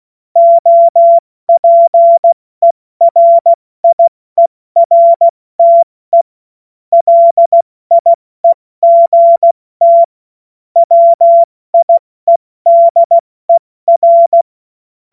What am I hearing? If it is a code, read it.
Morse code, heard as a keyed tone, about 16 wpm, 680 Hz. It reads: OPERIERTE LIEGT WIEDER
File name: Morsekurs / Witze / Witz 13 Februar 2024 70 F80